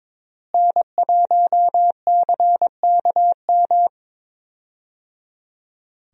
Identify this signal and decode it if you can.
Morse code, heard as a keyed tone, about 22 wpm, 700 Hz. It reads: N1CKM
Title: USA Call Signs - 5 Characters - Repeated 3x - 22wpm